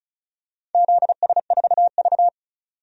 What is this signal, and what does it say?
Morse code, heard as a keyed tone, about 34 wpm, 700 Hz. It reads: ZS4V
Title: DXCC-100 Call Signs - 4 Characters - 35wpm